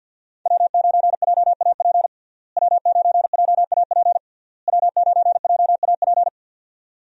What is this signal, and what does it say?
Morse code, heard as a keyed tone, about 50 wpm, 700 Hz. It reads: W9JAP W9JAP W9JAP
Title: USA Call Signs - 5 Characters - Repeated 3x - 50wpm